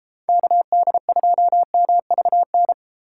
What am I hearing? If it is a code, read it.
Morse code, heard as a keyed tone, about 32 wpm, 700 Hz. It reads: KD2MVN